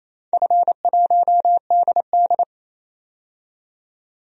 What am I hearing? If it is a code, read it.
Morse code, heard as a keyed tone, about 28 wpm, 700 Hz. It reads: F1DD
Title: DXCC-100 Call Signs - 4 Characters - 28wpm